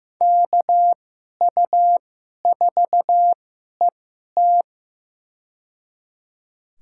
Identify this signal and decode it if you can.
Morse code, heard as a keyed tone, about 15 wpm, spaced as for 11 wpm, 700 Hz. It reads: KU4ET
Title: USA Call Signs - 2x Character Spacing - 15wpm